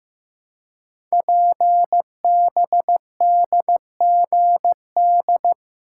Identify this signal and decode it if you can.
Morse code, heard as a keyed tone, about 15 wpm, 700 Hz. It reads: PBDGD
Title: Code Groups - 5 Letters - 15wpm